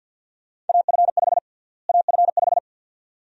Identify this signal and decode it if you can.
Morse code, heard as a keyed tone, about 47 wpm, 700 Hz. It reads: AF5 AF5
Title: USA Call Sign Prefixes - Repeated 2x - 50wpm